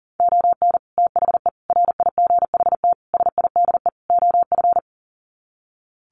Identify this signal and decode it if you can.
Morse code, heard as a keyed tone, about 40 wpm, 700 Hz. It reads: ON THE RIGHT SIDE OF
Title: Sets of 5 Words Encore 40wpm